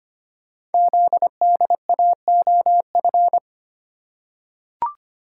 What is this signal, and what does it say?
Morse code, heard as a keyed tone, about 25 wpm, 700 Hz. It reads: ZDAOF
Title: Code Groups - 5 Letter:Numbers - 25wpm